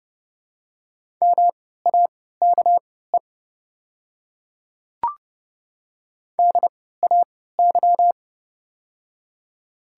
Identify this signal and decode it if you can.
Morse code, heard as a keyed tone, about 30 wpm, spaced as for 17 wpm, 700 Hz. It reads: MAKE DAY
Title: 3x Character Spacing - Top 100 Words - 30wpm